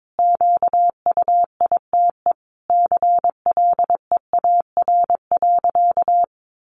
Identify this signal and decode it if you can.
Morse code, heard as a keyed tone, about 22 wpm, 700 Hz. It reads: QUITE CLEAR.
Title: Sentences from Top 1100 Words 22wpm